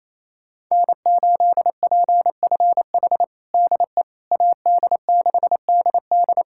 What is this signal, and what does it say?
Morse code, heard as a keyed tone, about 28 wpm, 700 Hz. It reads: N8PFH DE AD6DD
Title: USA Callsign DE Callsign - 5 Characters - 28wpm